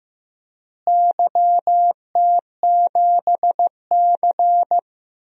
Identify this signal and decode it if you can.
Morse code, heard as a keyed tone, about 15 wpm, 700 Hz. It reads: YT7C
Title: DXCC-100 Call Signs - 4 Characters - Repeated 3x - 15wpm